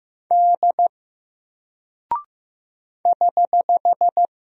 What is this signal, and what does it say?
Morse code, heard as a keyed tone, about 15 wpm, 700 Hz. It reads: D <HH>
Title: Single Character (Letter:Number:Punctuation:Prosign) Alt 15wpm